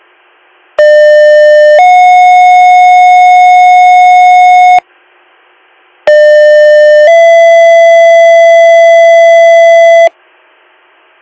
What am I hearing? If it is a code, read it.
Two-tone page, heard as a one-second tone, then a three-second tone, about 600 then 730 Hz, about 600 then 670 Hz. It is heard from 0.8 to 10.1 s.